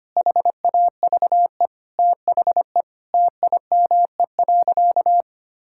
Morse code, heard as a keyed tone, about 25 wpm, 700 Hz. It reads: HAVE THE TIME.
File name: practice / 25wpm / Sentences with 4-Word Phrases - 25wpm